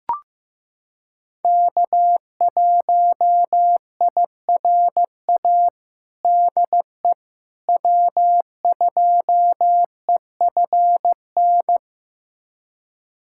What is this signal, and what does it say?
Morse code, heard as a keyed tone, about 15 wpm, 700 Hz. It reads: K1IRA DE W2EFN
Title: USA Callsign DE Callsign - 5 Characters - 15wpm